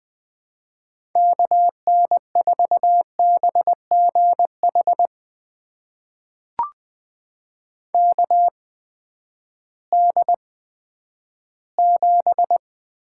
Morse code, heard as a keyed tone, about 20 wpm, spaced as for 11 wpm, 700 Hz. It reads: KN4BGH KD7
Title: USA Call Signs - 6 Characters - 8x Character Spacing - 20wpm